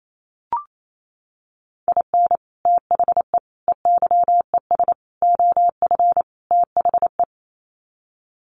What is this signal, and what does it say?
Morse code, heard as a keyed tone, about 28 wpm, 700 Hz. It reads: IN THE EYES OF THE
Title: Sets of 5 Words 28wpm